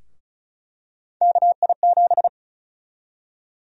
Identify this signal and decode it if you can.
Morse code, heard as a keyed tone, about 34 wpm, 700 Hz. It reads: KI7
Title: USA Call Sign Prefixes - Repeated 2x - 35wpm